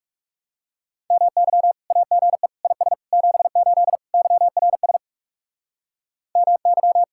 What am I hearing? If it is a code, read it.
Morse code, heard as a keyed tone, about 36 wpm, 700 Hz. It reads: MY AGE IS 78 YRS MY
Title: Speed-Racing Common QSO Phrases - 30wpm